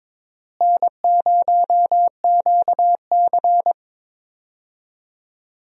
Morse code, heard as a keyed tone, about 22 wpm, 700 Hz. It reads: N0QC
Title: USA Call Signs - 4 Characters - Repeated 3x - 22wpm